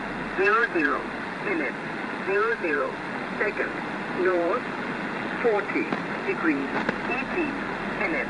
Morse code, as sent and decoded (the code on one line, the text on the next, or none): none